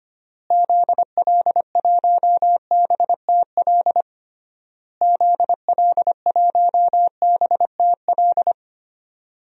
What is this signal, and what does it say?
Morse code, heard as a keyed tone, about 25 wpm, 700 Hz. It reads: ZL1BTL ZL1BTL
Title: DXCC-100 Call Signs - 6 Characters - Repeated 3x - 25wpm